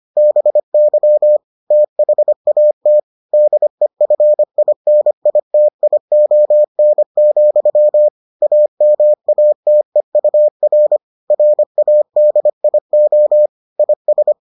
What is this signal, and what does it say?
Morse code, heard as a keyed tone, about 25 wpm, 600 Hz. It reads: BY THAT DEFINITION, AMATEUR RADIO IS